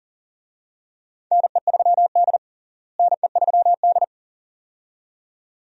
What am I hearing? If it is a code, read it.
Morse code, heard as a keyed tone, about 40 wpm, 700 Hz. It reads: NE3D NE3D
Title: USA Call Signs - 4 Characters - Repeated 2x - 40wpm